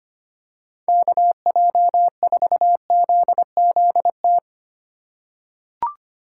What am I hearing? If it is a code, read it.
Morse code, heard as a keyed tone, about 25 wpm, 700 Hz. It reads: KJ4ZZT
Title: USA Call Signs - 6 Characters - 8x Character Spacing - 25wpm